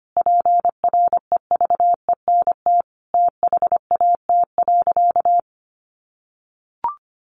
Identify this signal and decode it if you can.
Morse code, heard as a keyed tone, about 25 wpm, 700 Hz. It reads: PREVENT THAT.
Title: Sentences from Top 1100 Words 25wpm